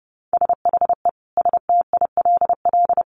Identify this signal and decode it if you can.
Morse code, heard as a keyed tone, about 30 wpm, 700 Hz. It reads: SHE STILL
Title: Additional Easy Phrases 30wpm